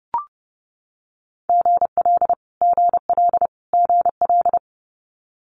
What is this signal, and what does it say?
Morse code, heard as a keyed tone, about 30 wpm, 700 Hz. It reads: GL GL GL
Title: Multi-letter Phonemes Repeated 3x 30wpm